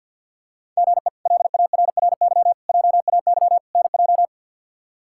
Morse code, heard as a keyed tone, about 53 wpm, 700 Hz. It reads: GE LARRY JAY NJ
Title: K1USN SST QSO Practice - 50wpm